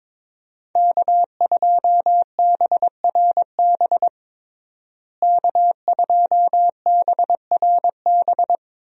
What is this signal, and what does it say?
Morse code, heard as a keyed tone, about 22 wpm, 700 Hz. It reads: K2BRB K2BRB
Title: USA Call Signs - 5 Characters - Repeated 2x - 22wpm